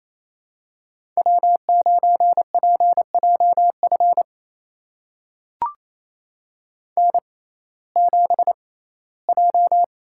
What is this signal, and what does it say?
Morse code, heard as a keyed tone, about 28 wpm, spaced as for 17 wpm, 700 Hz. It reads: W9PJF N7J
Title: USA Call Signs - 5 Characters - 6x Character Spacing - 28wpm